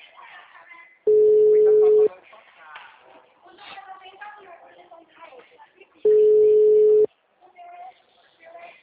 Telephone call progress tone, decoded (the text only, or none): ringback tone